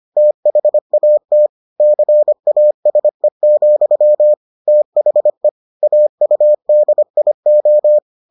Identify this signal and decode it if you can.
Morse code, heard as a keyed tone, about 25 wpm, 600 Hz. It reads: THAT CASE, THE AUDIO